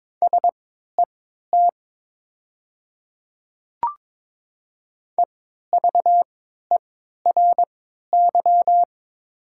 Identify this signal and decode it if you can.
Morse code, heard as a keyed tone, about 22 wpm, spaced as for 13 wpm, 700 Hz. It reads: SET EVERY
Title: 3x Character Spacing - Top 200 Words - 22wpm